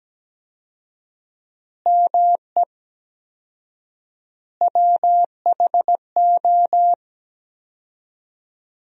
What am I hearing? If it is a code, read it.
Morse code, heard as a keyed tone, about 17 wpm, 700 Hz. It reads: ME WHO
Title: 4x Word-Spacing - Sentences from Top 200 Words - 17wpm